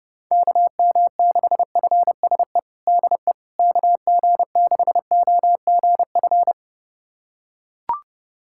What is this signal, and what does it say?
Morse code, heard as a keyed tone, about 30 wpm, 700 Hz. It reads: KM6FSE DE KG6OGF
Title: USA Callsign DE Callsign - 6 Characters - 30wpm